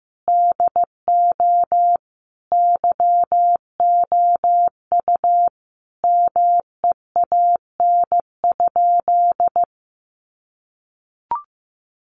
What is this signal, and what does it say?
Morse code, heard as a keyed tone, about 15 wpm, 700 Hz. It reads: DO YOU MEAN?
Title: Sentences from Top 1300 Words 15wpm